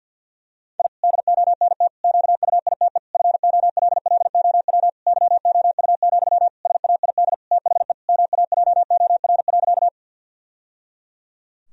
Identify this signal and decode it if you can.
Morse code, heard as a keyed tone, about 47 wpm, 700 Hz. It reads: I DONT QUITE FOLLOW YOU, SAID THE MAJOR.
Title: Sentences from Top 1000 Words - Review Part 4 - 50wpm